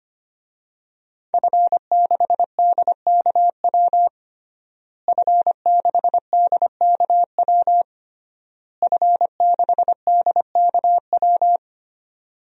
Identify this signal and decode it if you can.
Morse code, heard as a keyed tone, about 25 wpm, 700 Hz. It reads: F6DKW F6DKW F6DKW
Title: DXCC-100 Call Signs - 5 Characters - Repeated 3x - 25wpm